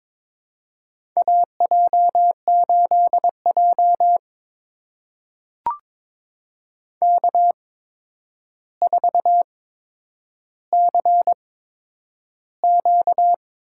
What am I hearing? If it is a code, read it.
Morse code, heard as a keyed tone, about 22 wpm, spaced as for 9 wpm, 700 Hz. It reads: AJ8J K4CQ
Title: USA Call Signs - 4 Characters - 8x Character Spacing - 22wpm